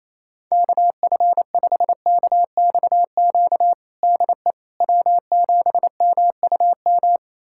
Morse code, heard as a keyed tone, about 28 wpm, 700 Hz. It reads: KF5KXQ DE W7MUM
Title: USA Callsign DE Callsign 28wpm